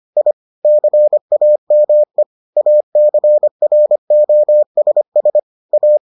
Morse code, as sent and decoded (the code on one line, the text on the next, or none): .. / -.-. .- -- . / .- -.-. .-. --- ... ... / .-
I CAME ACROSS A